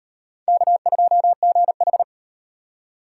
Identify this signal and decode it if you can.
Morse code, heard as a keyed tone, about 38 wpm, 700 Hz. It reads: K2GH